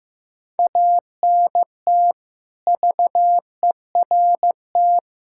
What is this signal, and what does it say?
Morse code, heard as a keyed tone, about 15 wpm, 700 Hz. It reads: ANT VERT
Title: K1USN SST QSO Practice 15wpm